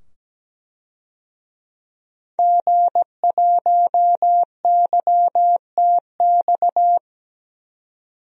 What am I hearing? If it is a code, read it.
Morse code, heard as a keyed tone, about 17 wpm, 700 Hz. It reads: G1YTX
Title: DXCC-100 Call Signs - 5 Characters - 17wpm